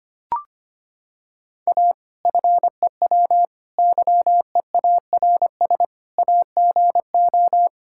Morse code, heard as a keyed tone, about 25 wpm, 700 Hz. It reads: A FEW YEARS AGO
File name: practice / 25wpm / Sentences with 4-Word Phrases - 25wpm